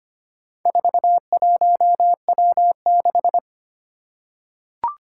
Morse code, heard as a keyed tone, about 25 wpm, 700 Hz. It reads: 41W6
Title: Four Challenging Letter-Number Combinations 25wpm